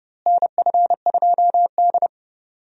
Morse code, heard as a keyed tone, about 30 wpm, 700 Hz. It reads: NF2D
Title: USA Call Signs - 4 Characters - Repeated 3x - 30wpm